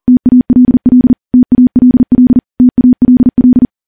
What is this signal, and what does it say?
Morse code, heard as a keyed tone, about 40 wpm, 250 Hz. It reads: TALL TALL TALL
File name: morse / morse-training-40WPM